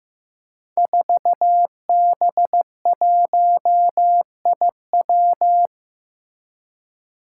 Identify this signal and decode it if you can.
Morse code, heard as a keyed tone, about 16 wpm, 700 Hz. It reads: 4B1IW
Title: Five Challenging Letter-Number Combinations 15wpm